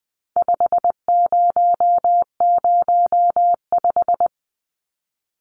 Morse code, heard as a keyed tone, about 20 wpm, 700 Hz. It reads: 5005
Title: Numbers - 4-Digit 20wpm